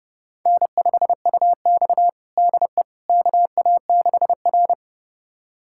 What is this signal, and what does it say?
Morse code, heard as a keyed tone, about 32 wpm, 700 Hz. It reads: N5UX DE KA6R
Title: USA Callsign DE Callsign - 4 Characters - 30wpm